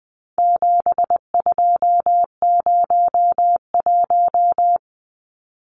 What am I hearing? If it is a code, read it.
Morse code, heard as a keyed tone, about 20 wpm, 700 Hz. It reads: 7201
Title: Numbers - 4-Digit 20wpm